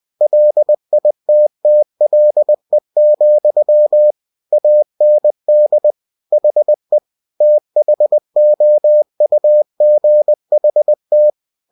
Morse code, heard as a keyed tone, about 20 wpm, 600 Hz. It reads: LITTLE, AND HE THOUGHT
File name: cw-books / winnie-the-pooh / winnie-20wpm-page-0014